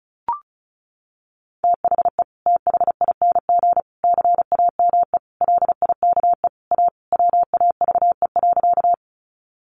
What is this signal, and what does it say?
Morse code, heard as a keyed tone, about 34 wpm, 700 Hz. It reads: THE THING CAME LIKE A WAVE.
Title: Sentences from Top 1400 Words 35wpm